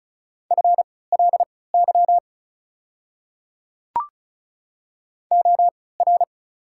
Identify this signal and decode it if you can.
Morse code, heard as a keyed tone, about 34 wpm, spaced as for 20 wpm, 690 Hz. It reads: FLY OR
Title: 3x Character Spacing - Top 500 Words - 35wpm